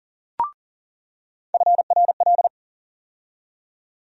Morse code, heard as a keyed tone, about 40 wpm, 700 Hz. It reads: FRL
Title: USA Call Sign Postfixes - 3 Letters 40wpm